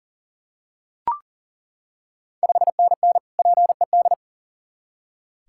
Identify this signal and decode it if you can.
Morse code, heard as a keyed tone, about 39 wpm, 700 Hz. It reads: 5NN PED
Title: Ontario QSO Party Exchange 40wpm